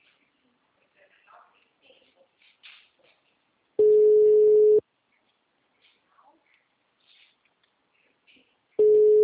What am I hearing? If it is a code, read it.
Ringback tone (Europe, 425 Hz), with speech.
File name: out- -2002- - - .359